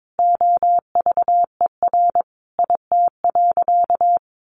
Morse code, heard as a keyed tone, about 22 wpm, 700 Hz. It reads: OVER IT.
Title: Sentences from Top 1500 Words 22wpm